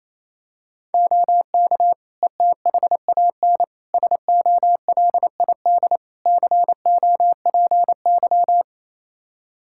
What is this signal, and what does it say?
Morse code, heard as a keyed tone, about 28 wpm, 700 Hz. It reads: OK ETHAN SOLID COPY
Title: Practice QSO - Exchange 3 - More Info - 28wpm